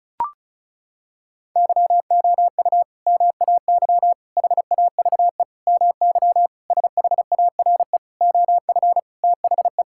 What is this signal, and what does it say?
Morse code, heard as a keyed tone, about 36 wpm, 700 Hz. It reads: YOU MAY HAVE MY SHARE OF THE
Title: Sentences from Top 1000 Words - Review Part 6 - 35wpm